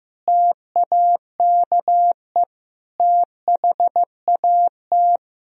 Morse code, heard as a keyed tone, about 15 wpm, 700 Hz. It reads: TAKE THAT
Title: Sentences from Top 1000 Words - Review Part 4 - 15wpm